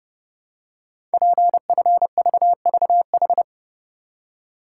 Morse code, heard as a keyed tone, about 30 wpm, 700 Hz. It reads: PFVVH
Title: Code Groups - 5 Letters - 30wpm